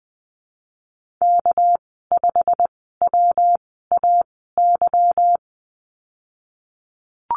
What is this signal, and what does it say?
Morse code, heard as a keyed tone, about 20 wpm, spaced as for 15 wpm, 700 Hz. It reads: K5WAY
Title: 213 Lesson 34g - Number 8 - Call Signs 20wpm